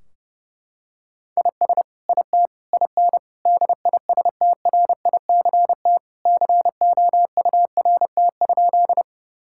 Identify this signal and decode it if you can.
Morse code, heard as a keyed tone, about 30 wpm, 700 Hz. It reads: IS IT IN DISTRICT COURT?
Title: Sentences from Top 1600 Words 30wpm